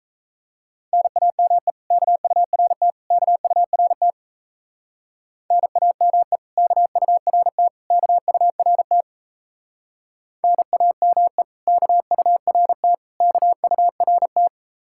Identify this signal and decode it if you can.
Morse code, heard as a keyed tone, about 37 wpm, 700 Hz. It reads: NAME KURT KURT NAME KURT KURT NAME KURT KURT